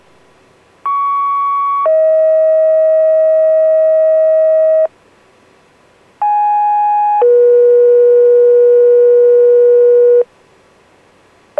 Two-tone page, heard as a one-second tone, then a three-second tone, about 1120 then 640 Hz, about 830 then 480 Hz. It is heard from 0.8 to 10.2 s.
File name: PAGE / PAGING-5L